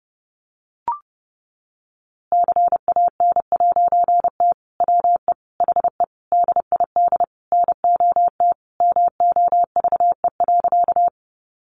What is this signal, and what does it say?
Morse code, heard as a keyed tone, about 30 wpm, 700 Hz. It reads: CAN'T WE HE DID NOT MOVE.
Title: A Sound Of Thunder 30wpm